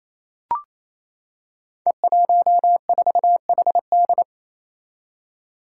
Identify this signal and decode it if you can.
Morse code, heard as a keyed tone, about 29 wpm, 700 Hz. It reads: E14HD
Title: Five Challenging Letter-Number Combinations 28wpm